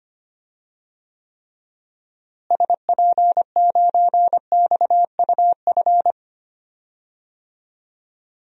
Morse code, heard as a keyed tone, about 25 wpm, 700 Hz. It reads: SP9XUF
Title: DXCC-100 Call Signs - 6 Characters - Repeated 2x - 25wpm